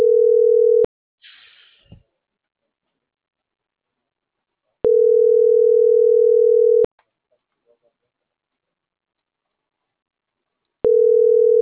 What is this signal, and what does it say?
Ringback tone (North America), 440 + 480 Hz.
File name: CENTRAL MAERSK / out polanco.msk